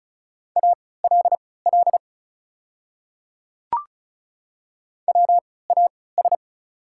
Morse code, heard as a keyed tone, about 36 wpm, spaced as for 20 wpm, 700 Hz. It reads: ALL WAS
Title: 3x Character Spacing - Top 200 Words - 35wpm